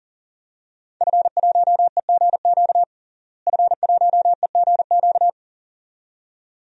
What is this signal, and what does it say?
Morse code, heard as a keyed tone, about 40 wpm, 700 Hz. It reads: F1EGQ F1EGQ
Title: DXCC-100 Call Signs - 5 Characters - Repeated 2x - 40wpm